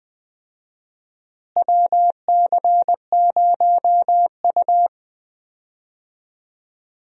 Morse code, heard as a keyed tone, about 20 wpm, 700 Hz. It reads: WC0U
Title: USA Call Signs - 4 Characters - Repeated 2x - 20wpm